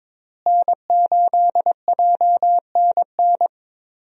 Morse code, heard as a keyed tone, about 22 wpm, 700 Hz. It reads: N8JNN
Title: USA Call Signs - 5 Characters - Repeated 3x - 22wpm